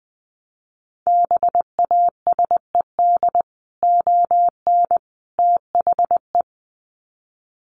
Morse code, heard as a keyed tone, about 20 wpm, 700 Hz. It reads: BASED ON THE
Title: Sets of 3 Words 20wpm